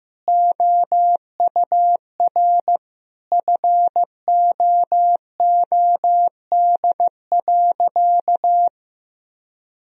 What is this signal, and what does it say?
Morse code, heard as a keyed tone, about 15 wpm, 700 Hz. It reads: OUR FOOD.
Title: Sentences from Top 1000 Words - Review Part 3 - 15wpm